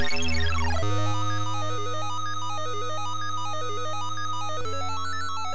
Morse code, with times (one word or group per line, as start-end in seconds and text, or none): none